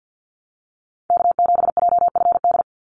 Morse code, heard as a keyed tone, about 50 wpm, 700 Hz. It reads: K7JFD
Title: NA SOTA Chaser Callsigns 50wpm